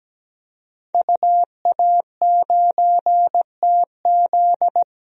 Morse code, heard as a keyed tone, about 17 wpm, 700 Hz. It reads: UA9TZ
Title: DXCC-100 Call Signs - 5 Characters - 17wpm